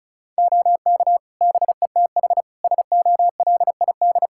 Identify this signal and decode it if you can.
Morse code, heard as a keyed tone, about 36 wpm, 700 Hz. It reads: OK BETH SOLID
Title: Practice QSO - Exchange 3 - More Info - 35wpm